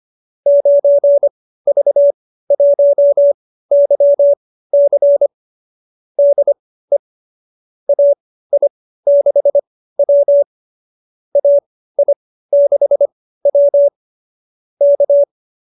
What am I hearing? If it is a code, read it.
Morse code, heard as a keyed tone, about 25 wpm, spaced as for 15 wpm, 590 Hz. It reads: 9V1YC DE AI6W AI6W K